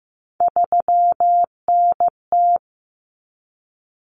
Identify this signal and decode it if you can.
Morse code, heard as a keyed tone, about 15 wpm, 700 Hz. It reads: 3NT
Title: Common Cut Numbers - 3-Digit 15wpm